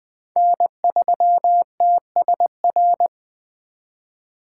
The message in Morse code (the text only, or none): N3TSR